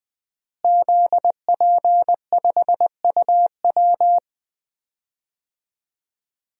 Morse code, heard as a keyed tone, about 20 wpm, 700 Hz. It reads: ZP5UW
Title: DXCC-100 Call Signs - 5 Characters - Repeated 2x - 20wpm